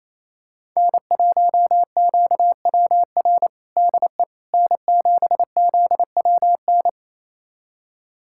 Morse code, heard as a keyed tone, about 28 wpm, 690 Hz. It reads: N1QWR DE N7ZWN
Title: USA Callsign DE Callsign - 5 Characters - 28wpm